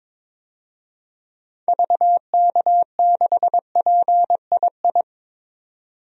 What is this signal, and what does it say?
Morse code, heard as a keyed tone, about 22 wpm, 700 Hz. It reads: VK6PII
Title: DXCC-100 Call Signs - 6 Characters - 22wpm